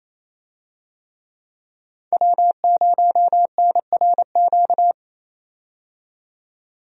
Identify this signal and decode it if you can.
Morse code, heard as a keyed tone, about 27 wpm, 700 Hz. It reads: W0NRQ
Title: USA Call Signs - 5 Characters - 28wpm